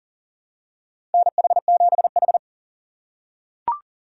Morse code, heard as a keyed tone, about 40 wpm, 700 Hz. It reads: NH7H
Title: USA Call Signs - 2x Character Spacing - 40wpm